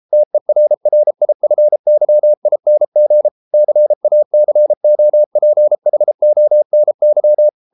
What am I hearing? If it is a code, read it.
Morse code, heard as a keyed tone, about 33 wpm, 600 Hz. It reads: TERRIFYING CACOPHONY